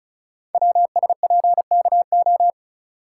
Morse code, heard as a keyed tone, about 36 wpm, 690 Hz. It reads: WSPKO
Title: Code Groups - 5 Letters - 35wpm